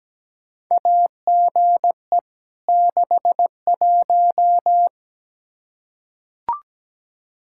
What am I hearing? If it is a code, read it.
Morse code, heard as a keyed tone, about 17 wpm, 700 Hz. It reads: AGE 61
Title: K1USN SST QSO Practice 17wpm